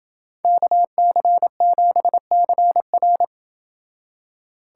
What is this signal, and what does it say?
Morse code, heard as a keyed tone, about 27 wpm, 700 Hz. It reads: KC7CR